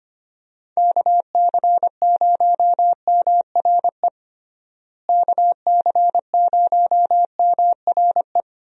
Morse code, heard as a keyed tone, about 25 wpm, 700 Hz. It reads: KC0MRE KC0MRE
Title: USA Call Signs - 6 Characters - Repeated 2x - 25wpm